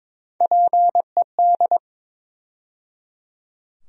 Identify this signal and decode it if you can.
Morse code, heard as a keyed tone, about 22 wpm, 700 Hz. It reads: PED
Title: USA Call Sign Postfixes - 3 Letters 22wpm